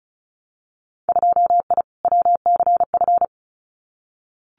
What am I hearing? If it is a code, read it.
Morse code, heard as a keyed tone, about 36 wpm, 700 Hz. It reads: 2I WCF
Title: Winter Field Day Exchanges 35wpm